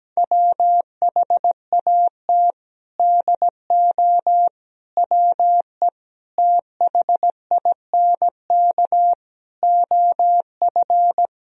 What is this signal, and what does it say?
Morse code, heard as a keyed tone, about 17 wpm, 700 Hz. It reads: WHAT DO WE THINK OF
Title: Sentences from Top 1000 Words - Review Part 2 - 17wpm